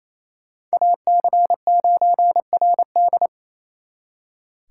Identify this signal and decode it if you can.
Morse code, heard as a keyed tone, about 36 wpm, 700 Hz. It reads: AC9RD